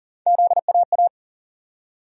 Morse code, heard as a keyed tone, about 39 wpm, 700 Hz. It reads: ZAA
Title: USA Call Sign Postfixes - 3 Letters 40wpm